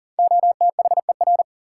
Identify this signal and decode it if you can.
Morse code, heard as a keyed tone, about 40 wpm, 700 Hz. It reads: OTHER
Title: Sentences from Top 100 Words - New Word Drill - 40wpm